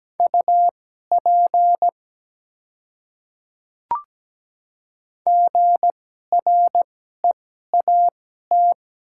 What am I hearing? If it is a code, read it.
Morse code, heard as a keyed tone, about 17 wpm, spaced as for 12 wpm, 700 Hz. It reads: UP GREAT
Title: 2x Character Spacing - Top 200 Words - 17wpm